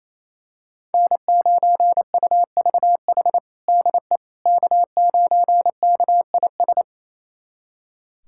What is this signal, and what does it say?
Morse code, heard as a keyed tone, about 27 wpm, 700 Hz. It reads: N9UVH DE K9KIS
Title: USA Callsign DE Callsign - 5 Characters - 28wpm